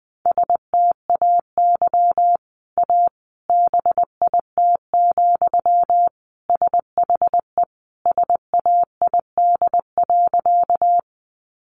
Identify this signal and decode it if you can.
Morse code, heard as a keyed tone, about 20 wpm, 700 Hz. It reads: STAY A BIT, SHE SAID.
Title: Sentences from Top 900 Words 20wpm